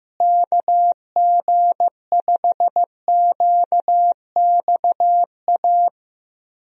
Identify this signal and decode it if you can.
Morse code, heard as a keyed tone, about 15 wpm, 700 Hz. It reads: KG5QXA
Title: USA Call Signs - 6 Characters - Repeated 3x - 15wpm